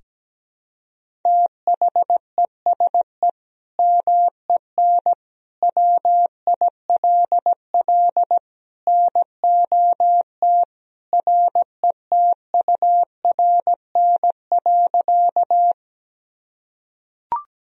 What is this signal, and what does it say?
Morse code, heard as a keyed tone, about 17 wpm, 700 Hz. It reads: THESE MEN WILL NOT RETURN.
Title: Sentences from Top 1000 Words - Review Part 1 - 17wpm